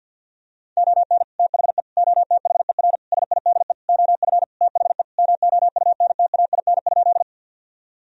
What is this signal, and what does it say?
Morse code, heard as a keyed tone, about 52 wpm, 700 Hz. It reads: ON THE OTHER SIDE OF THE MOUNTAIN?
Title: Sentences from Top 1000 Words - Review Part 1 - 50wpm